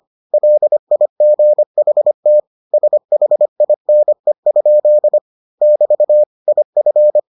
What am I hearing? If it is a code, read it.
Morse code, heard as a keyed tone, about 25 wpm, 600 Hz. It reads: LIGHT SHINE? <BT> IF